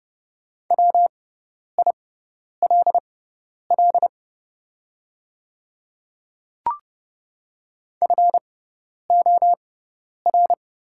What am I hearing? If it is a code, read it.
Morse code, heard as a keyed tone, about 30 wpm, spaced as for 10 wpm, 700 Hz. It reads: WILL FOR
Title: 6x Character Spacing - Top 100 Words - 30wpm